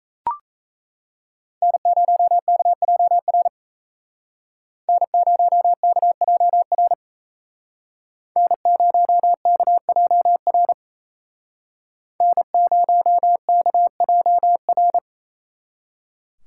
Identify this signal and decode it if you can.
Morse code, heard as a keyed tone, about 36 wpm, 700 Hz. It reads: N0KJR N0KJR N0KJR N0KJR